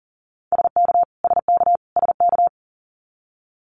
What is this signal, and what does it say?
Morse code, heard as a keyed tone, about 40 wpm, 700 Hz. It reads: SK SK SK
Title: Canadian Provinces and Territories - Repeated 3x - 40wpm